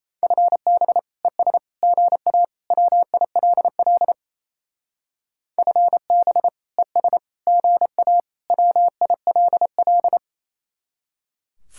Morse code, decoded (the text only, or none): FB ES GA WILL FB ES GA WILL